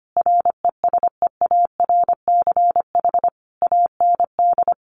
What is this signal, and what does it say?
Morse code, heard as a keyed tone, about 25 wpm, 700 Hz. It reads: RESEARCH AND
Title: Sentences from Top 1400 Words 25wpm